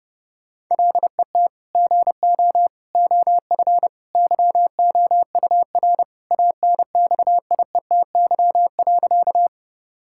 Morse code, heard as a keyed tone, about 30 wpm, 700 Hz. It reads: LET GO OF YOUR ANXIETY.
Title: Sentences from Top 1600 Words 30wpm